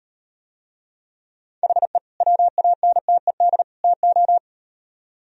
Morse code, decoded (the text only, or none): HE WANTED TO